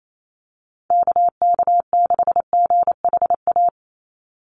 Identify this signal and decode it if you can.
Morse code, heard as a keyed tone, about 28 wpm, 700 Hz. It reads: KK6GHA
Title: Call Signs - 28wpm